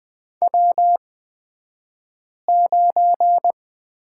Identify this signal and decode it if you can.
Morse code, heard as a keyed tone, about 20 wpm, spaced as for 5 wpm, 700 Hz. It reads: W9
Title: 249 Lesson 40d - BK - Two Characters 20wpm